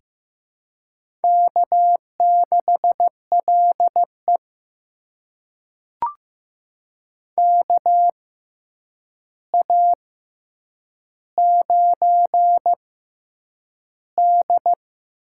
Morse code, heard as a keyed tone, about 15 wpm, spaced as for 8 wpm, 700 Hz. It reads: K6LE KA9D
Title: USA Call Signs - 4 Characters - 6x Character Spacing - 15wpm